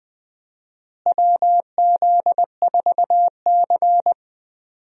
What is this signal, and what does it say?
Morse code, heard as a keyed tone, about 20 wpm, 700 Hz. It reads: WZ4C
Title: USA Call Signs - 4 Characters - Repeated 2x - 20wpm